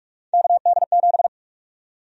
Morse code, heard as a keyed tone, about 46 wpm, 700 Hz. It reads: KD7
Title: USA Call Sign Prefixes - 45wpm